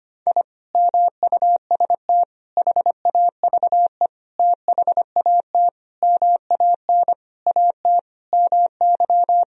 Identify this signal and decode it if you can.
Morse code, heard as a keyed tone, about 25 wpm, 700 Hz. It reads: I MUST HAVE THAT MAN AT MY
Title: Sentences from Top 1000 Words - Review Part 2 - 25wpm